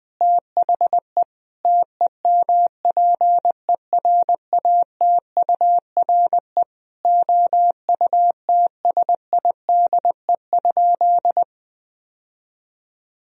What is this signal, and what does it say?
Morse code, heard as a keyed tone, about 21 wpm, 700 Hz. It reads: THE TEMPERATURE OUTSIDE?
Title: Sentences from Top 1000 Words - Review Part 4 - 20wpm